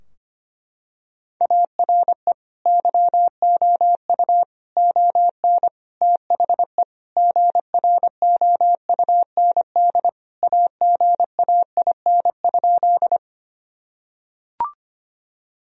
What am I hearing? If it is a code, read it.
Morse code, heard as a keyed tone, about 25 wpm, 700 Hz. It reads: ARE YOU ON THE GROUND AGAIN?
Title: Sentences from Top 1000 Words - Review Part 2 - 25wpm